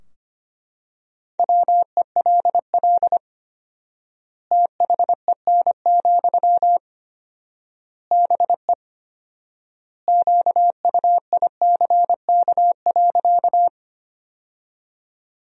Morse code, heard as a keyed tone, about 25 wpm, 700 Hz. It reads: WELL THEN, BE QUICK.
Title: 4x Word-Spacing - Sentences from Top 500 Words - 25wpm